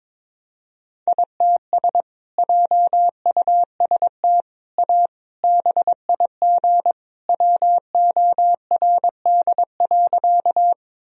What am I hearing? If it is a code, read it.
Morse code, heard as a keyed tone, about 21 wpm, 700 Hz. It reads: ITS JUST A BIG WORD.
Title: Sentences from Top 1000 Words - Review Part 1 - 22wpm